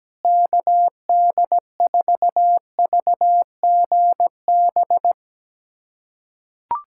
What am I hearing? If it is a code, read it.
Morse code, heard as a keyed tone, about 17 wpm, 700 Hz. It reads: KD4VGB
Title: USA Callsign DE Callsign 17wpm